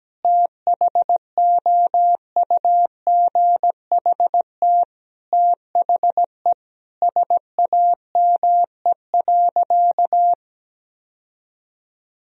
Morse code, heard as a keyed tone, about 17 wpm, 700 Hz. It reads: THOUGHT THE SAME.
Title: Sentences from Top 1000 Words - Review Part 1 - 17wpm